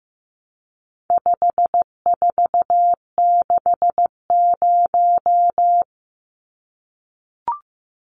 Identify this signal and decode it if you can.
Morse code, heard as a keyed tone, about 14 wpm, 700 Hz. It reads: 5460
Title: Numbers - 4-Digit 15wpm